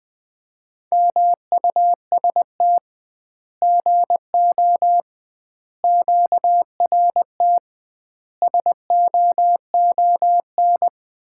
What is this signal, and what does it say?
Morse code, heard as a keyed tone, about 20 wpm, 700 Hz. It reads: MUST GO QRT SOON
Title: Common QSO Phrases - 2x Word-Spacing - 20wpm